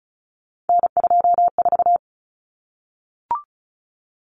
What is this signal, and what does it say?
Morse code, heard as a keyed tone, about 34 wpm, 700 Hz. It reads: N24
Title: Common Cut Numbers - 3-Digit 35wpm